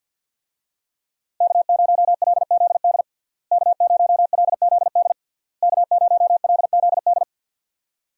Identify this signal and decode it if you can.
Morse code, heard as a keyed tone, about 52 wpm, 700 Hz. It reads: K0LZD K0LZD K0LZD
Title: USA Call Signs - 5 Characters - Repeated 3x - 50wpm